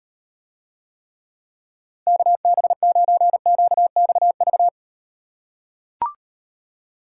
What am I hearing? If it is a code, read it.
Morse code, heard as a keyed tone, about 38 wpm, 700 Hz. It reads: KB9QXV